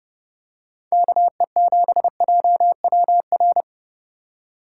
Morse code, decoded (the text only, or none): KE7JWR